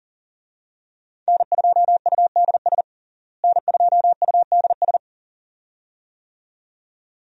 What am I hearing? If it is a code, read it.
Morse code, heard as a keyed tone, about 40 wpm, 700 Hz. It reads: N2UDS N2UDS
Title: USA Call Signs - 5 Characters - Repeated 2x - 40wpm